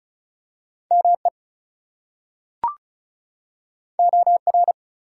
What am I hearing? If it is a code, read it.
Morse code, heard as a keyed tone, about 36 wpm, 700 Hz. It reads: ME OR
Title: Two Letter Words - 35wpm